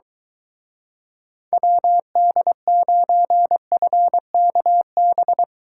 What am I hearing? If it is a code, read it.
Morse code, heard as a keyed tone, about 23 wpm, 700 Hz. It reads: WD9FKB